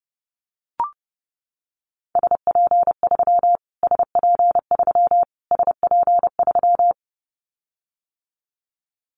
Morse code, heard as a keyed tone, about 30 wpm, 700 Hz. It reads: SP3 SP3 SP3
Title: Callsign Prefixes for Top 25 DXCC - Repeated 3x - 30wpm